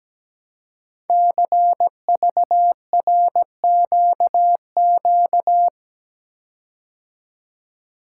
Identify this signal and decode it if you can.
Morse code, heard as a keyed tone, about 17 wpm, 700 Hz. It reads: CVRQQ
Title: Code Groups - 5 Letters - 17wpm